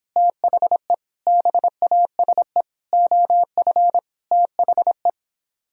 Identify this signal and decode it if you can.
Morse code, heard as a keyed tone, about 29 wpm, 700 Hz. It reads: THE BASE OF THE